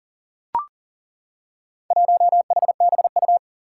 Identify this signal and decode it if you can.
Morse code, heard as a keyed tone, about 40 wpm, 700 Hz. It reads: 1HBU
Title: Four Challenging Letter-Number Combinations 40wpm